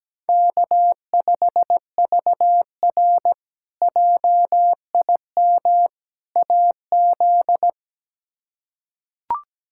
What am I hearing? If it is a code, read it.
Morse code, heard as a keyed tone, about 17 wpm, 700 Hz. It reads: K5VR JIM AZ
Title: K1USN SST QSO Practice - 17wpm